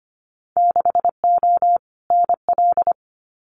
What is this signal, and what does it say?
Morse code, heard as a keyed tone, about 25 wpm, 700 Hz. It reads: 6O NL
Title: Winter Field Day Exchanges Encore 25wpm